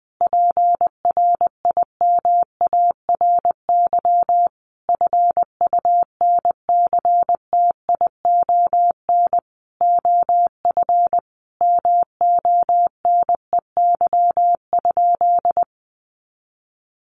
Morse code, heard as a keyed tone, about 20 wpm, 700 Hz. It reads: PRIMARY FUNCTION OF MONEY?
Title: Sentences from Top 1100 Words 20wpm